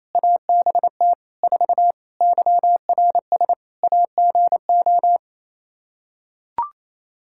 Morse code, heard as a keyed tone, about 27 wpm, 700 Hz. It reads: ABT 4 YRS AGO
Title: Practice QSO - Exchange 3 - More Info - 28wpm